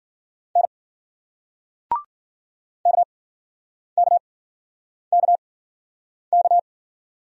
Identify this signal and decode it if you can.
Morse code, heard as a keyed tone, about 52 wpm, spaced as for 9 wpm, 700 Hz. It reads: N KKKK